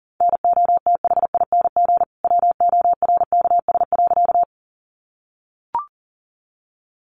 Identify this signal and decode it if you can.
Morse code, heard as a keyed tone, about 40 wpm, 700 Hz. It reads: NOTHING WORKS.
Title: Additional Easy Phrases 40wpm